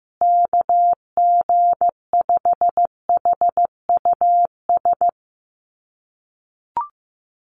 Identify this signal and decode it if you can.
Morse code, heard as a keyed tone, about 15 wpm, 700 Hz. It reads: KG5HUS
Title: USA Call Signs Repeated 3x 15wpm